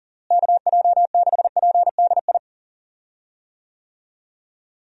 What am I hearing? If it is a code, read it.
Morse code, heard as a keyed tone, about 38 wpm, 700 Hz. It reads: KJ6PDI
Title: USA Call Signs - 6 Characters - Repeated 3x - 40wpm